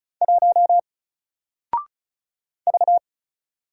Morse code, heard as a keyed tone, about 35 wpm, 700 Hz. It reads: 1 V
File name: Single-Letter-Number-35wpm